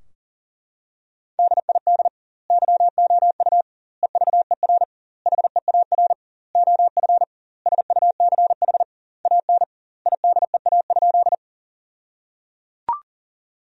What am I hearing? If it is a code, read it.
Morse code, heard as a keyed tone, about 37 wpm, 700 Hz. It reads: DID YOU EVER HEAR OF SUCH AN IDEA?
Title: 2x Word-Spacing - Sentences from Top 400 Words - 40wpm